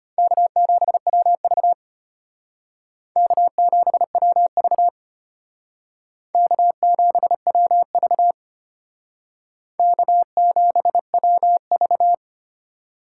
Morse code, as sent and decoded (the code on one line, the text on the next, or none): -.- --... .-- ...- / -.- --... .-- ...- / -.- --... .-- ...- / -.- --... .-- ...-
K7WV K7WV K7WV K7WV